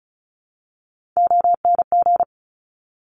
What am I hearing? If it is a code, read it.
Morse code, heard as a keyed tone, about 36 wpm, 700 Hz. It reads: ONG
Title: Top Three Letter Combinations in Words 35wpm